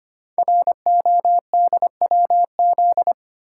Morse code, heard as a keyed tone, about 25 wpm, 700 Hz. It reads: RODWZ
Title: Code Groups - 5 Letters - 25wpm